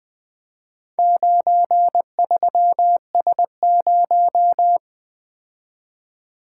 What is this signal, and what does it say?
Morse code, heard as a keyed tone, about 20 wpm, 700 Hz. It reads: 93S0
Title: Four Letter-Number Combinations - 20wpm